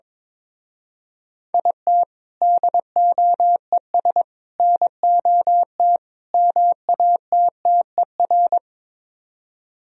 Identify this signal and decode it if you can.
Morse code, heard as a keyed tone, about 23 wpm, 700 Hz. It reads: IT DOES NOT MATTER
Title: Speed-Racing - Sets of 4 Words 22wpm